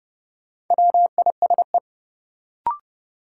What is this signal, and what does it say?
Morse code, heard as a keyed tone, about 30 wpm, 700 Hz. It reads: WISE
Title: Sentences from Top 1600 Words - New Word Drill - 30wpm